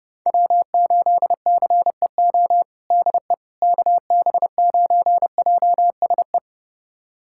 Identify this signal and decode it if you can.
Morse code, heard as a keyed tone, about 30 wpm, 700 Hz. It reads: W8CEO DE KB9JSE
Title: USA Callsign DE Callsign 30wpm